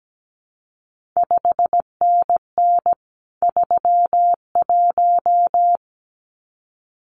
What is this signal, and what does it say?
Morse code, heard as a keyed tone, about 17 wpm, 700 Hz. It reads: 5NN 31
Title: CQ WW DX Contest Exchange - 17wpm